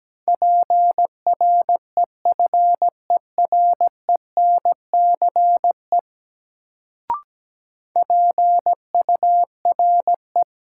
Morse code, heard as a keyed tone, about 17 wpm, 700 Hz. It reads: PREFERENCE PURE
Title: Sentences from Top 1500 Words - New Word Drill - 17wpm